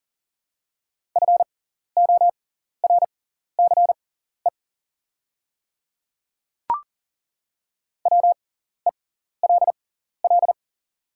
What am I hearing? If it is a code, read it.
Morse code, heard as a keyed tone, about 40 wpm, spaced as for 14 wpm, 700 Hz. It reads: FORCE WELL
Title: 6x Character Spacing - Top 500 Words - 40wpm